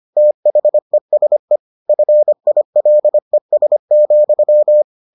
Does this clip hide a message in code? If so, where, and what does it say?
Yes: Morse code from 0.2 to 4.8 s, heard as a keyed tone, about 25 wpm, 600 Hz. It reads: THESE FILES,